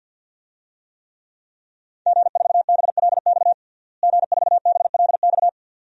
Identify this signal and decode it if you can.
Morse code, heard as a keyed tone, about 47 wpm, 700 Hz. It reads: G4BLX G4BLX
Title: DXCC-100 Call Signs - 5 Characters - Repeated 2x - 50wpm